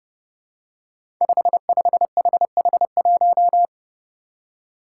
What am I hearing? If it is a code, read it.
Morse code, heard as a keyed tone, about 30 wpm, 700 Hz. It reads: 55HH1
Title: Five Challenging Letter-Number Combinations 30wpm